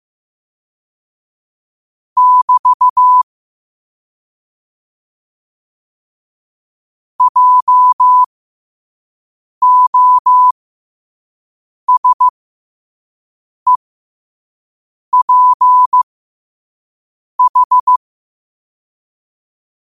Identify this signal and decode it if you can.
Morse code, heard as a keyed tone, about 15 wpm, spaced as for 5 wpm, 1000 Hz. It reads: <BT> JOSEPH
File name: 5wpm